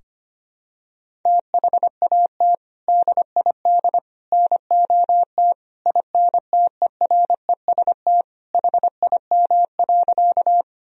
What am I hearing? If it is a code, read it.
Morse code, heard as a keyed tone, about 25 wpm, 700 Hz. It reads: THAT DID NOT INTEREST HIM.
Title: Sentences from Top 1000 Words - Review Part 2 - 25wpm